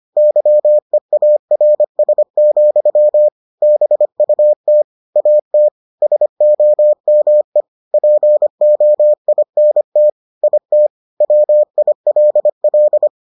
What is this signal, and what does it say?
Morse code, heard as a keyed tone, about 25 wpm, 600 Hz. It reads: YEARS, BUT AT SOME POINT IT WILL